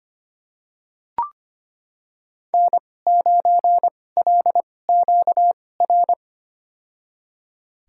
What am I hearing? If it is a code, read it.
Morse code, heard as a keyed tone, about 25 wpm, spaced as for 18 wpm, 700 Hz. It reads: N9LQR
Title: USA Call Signs - 5 Characters - 2x Character Spacing - 25wpm